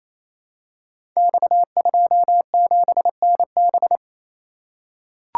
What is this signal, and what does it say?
Morse code, heard as a keyed tone, about 27 wpm, 700 Hz. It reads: X27NB
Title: Code Groups - 5 Letter:Numbers - 28wpm